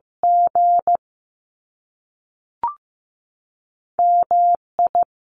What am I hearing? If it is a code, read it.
Morse code, heard as a keyed tone, about 15 wpm, 700 Hz. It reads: G M I
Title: Top Two Letter Combinations in Words 15wpm